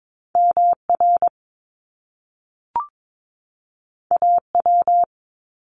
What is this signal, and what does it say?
Morse code, heard as a keyed tone, about 22 wpm, 700 Hz. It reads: MR AW
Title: Top Two Letter Combinations in Words 22wpm